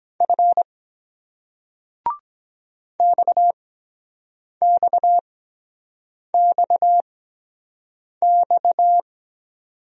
Morse code, heard as a keyed tone, about 23 wpm, spaced as for 7 wpm, 700 Hz. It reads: F XXXX